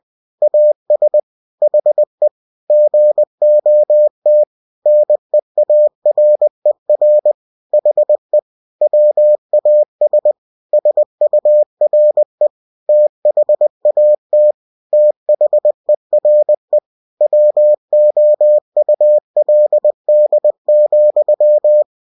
Morse code, heard as a keyed tone, about 20 wpm, 600 Hz. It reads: AS HE GOT NEARER HE WAS SURE THAT THERE WOULD,
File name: cw-books / winnie-the-pooh / winnie-20wpm-page-0041